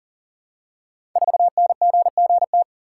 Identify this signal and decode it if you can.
Morse code, heard as a keyed tone, about 40 wpm, 700 Hz. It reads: 4NGGT
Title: Code Groups - 5 Letter:Numbers - 40wpm